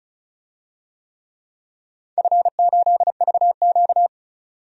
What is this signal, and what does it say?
Morse code, heard as a keyed tone, about 36 wpm, 700 Hz. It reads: F8VQ
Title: DXCC-100 Call Signs - 4 Characters - 35wpm